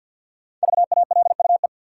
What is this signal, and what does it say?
Morse code, heard as a keyed tone, about 53 wpm, 700 Hz. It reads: VALUE
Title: Five Letter Words - 50wpm